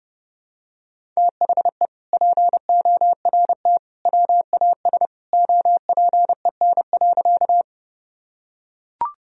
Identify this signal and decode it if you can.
Morse code, heard as a keyed tone, about 30 wpm, 700 Hz. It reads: THE PORT WAS OPEN.
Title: Sentences from Top 1000 Words - Review Part 1 - 30wpm